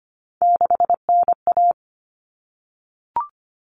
Morse code, heard as a keyed tone, about 25 wpm, 700 Hz. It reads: 6NA
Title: Common Cut Numbers - 3-Digit 25wpm